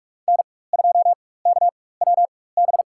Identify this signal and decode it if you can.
Morse code, heard as a keyed tone, about 46 wpm, spaced as for 21 wpm, 700 Hz. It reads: N2KWB
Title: USA Call Signs - 4x Character Spacing - 45wpm